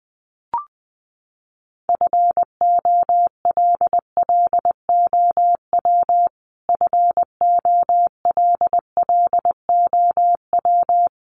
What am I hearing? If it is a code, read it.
Morse code, heard as a keyed tone, about 20 wpm, 700 Hz. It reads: FOLLOW FOLLOW
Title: Top 300 Words Repeated 20wpm